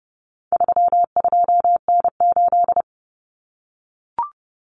Morse code, heard as a keyed tone, about 30 wpm, 700 Hz. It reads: 32N8
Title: Common Cut Numbers - 4-Digit 30wpm